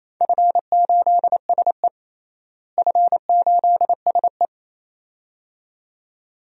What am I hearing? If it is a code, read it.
Morse code, heard as a keyed tone, about 27 wpm, 700 Hz. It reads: F8SE F8SE
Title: DXCC-100 Call Signs - 4 Characters - Repeated 2x - 28wpm